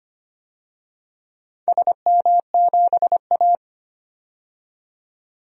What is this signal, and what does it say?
Morse code, heard as a keyed tone, about 25 wpm, 700 Hz. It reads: SM7A
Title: DXCC-100 Call Signs - 4 Characters - 25wpm